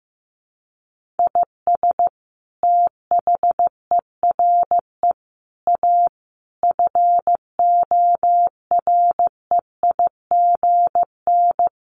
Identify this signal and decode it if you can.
Morse code, heard as a keyed tone, about 14 wpm, 700 Hz. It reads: IS THERE A FOREIGN
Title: Sentences from Top 1400 Words 15wpm